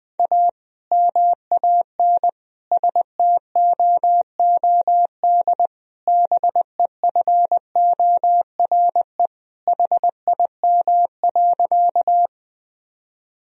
Morse code, heard as a keyed tone, about 20 wpm, 700 Hz. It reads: A MAN STOOD BEFORE HIM.
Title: Sentences from Top 1000 Words - Review Part 2 - 20wpm